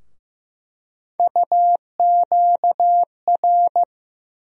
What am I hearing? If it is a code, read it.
Morse code, heard as a keyed tone, about 15 wpm, 700 Hz. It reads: UQR
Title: USA Call Sign Postfixes - 3 Letters 15wpm